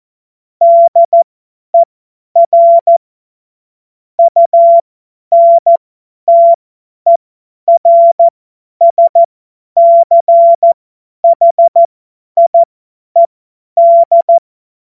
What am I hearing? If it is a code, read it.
Morse code, heard as a keyed tone, about 14 wpm, spaced as for 10 wpm, 680 Hz. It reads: DER UNTERSCHIED
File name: Morsekurs / Witze / Witz 31. Mai 2022 50 F70